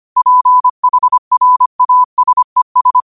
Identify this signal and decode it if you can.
Morse code, heard as a keyed tone, about 25 wpm, 1000 Hz. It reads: PHRASES
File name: qotdfiles / 25wpm